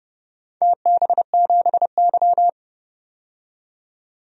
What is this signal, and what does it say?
Morse code, heard as a keyed tone, about 29 wpm, 700 Hz. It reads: TB7Y
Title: DXCC-100 Call Signs - 4 Characters - 30wpm